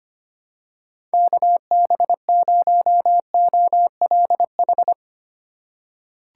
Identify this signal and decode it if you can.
Morse code, heard as a keyed tone, about 25 wpm, 700 Hz. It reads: KB0OLH
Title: USA Call Signs - 6 Characters - 2x Character Spacing - 25wpm